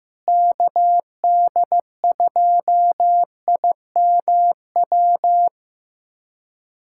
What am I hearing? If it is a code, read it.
Morse code, heard as a keyed tone, about 15 wpm, 700 Hz. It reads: KD2IMW
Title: USA Callsign DE Callsign 15wpm